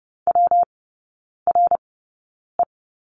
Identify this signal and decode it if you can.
Morse code, heard as a keyed tone, about 30 wpm, spaced as for 9 wpm, 700 Hz. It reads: WRE
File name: Lesson 19e - Letter w - Three Characters 30wpm